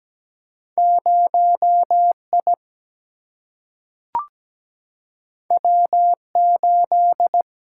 Morse code, heard as a keyed tone, about 17 wpm, 700 Hz. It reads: 0 I W 8
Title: Single-Letter Single-Number 17wpm